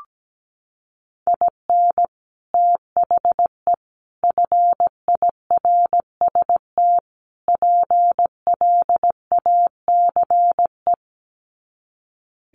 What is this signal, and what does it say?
Morse code, heard as a keyed tone, about 17 wpm, 700 Hz. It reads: IN THE FIRST PLACE
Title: Sets of 4 Words 17wpm